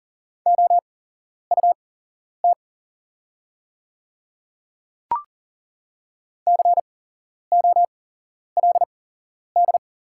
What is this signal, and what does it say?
Morse code, heard as a keyed tone, about 40 wpm, spaced as for 11 wpm, 690 Hz. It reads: OUT COLD
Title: 8x Character Spacing - Top 500 Words - 40wpm